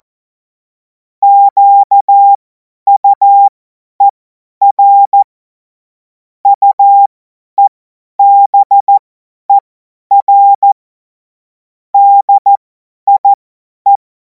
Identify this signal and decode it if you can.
Morse code, heard as a keyed tone, about 14 wpm, spaced as for 10 wpm, 800 Hz. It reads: QUER UEBER DIE
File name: Morsekurs / Witze / Witz 10. Februar 2026 50 F70